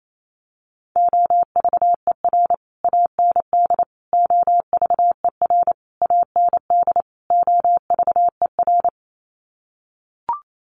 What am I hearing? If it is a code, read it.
Morse code, heard as a keyed tone, about 28 wpm, 700 Hz. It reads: OVER AND OVER AND OVER
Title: Sets of 5 Words Encore 28wpm